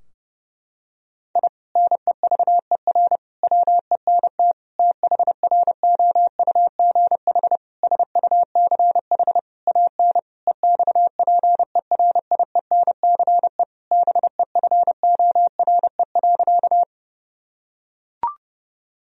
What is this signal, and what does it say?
Morse code, heard as a keyed tone, about 30 wpm, 700 Hz. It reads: I NEVER WENT THROUGH SUCH AN EXPERIENCE BEFORE.
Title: Sentences from Top 1000 Words - Review Part 3 - 30wpm